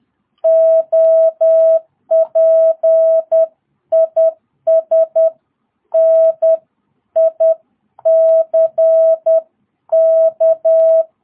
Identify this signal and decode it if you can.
Morse code, heard as a keyed tone, about 10 wpm, 660 Hz. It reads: OP IS NICK
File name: jh8cbh / 2022cw